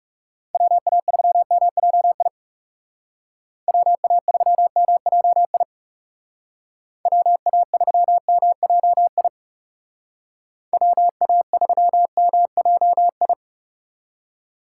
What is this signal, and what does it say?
Morse code, heard as a keyed tone, about 36 wpm, 700 Hz. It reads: WA3MJI WA3MJI WA3MJI WA3MJI
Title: Speed Racing - USA 2x3 Call Signs 30wpm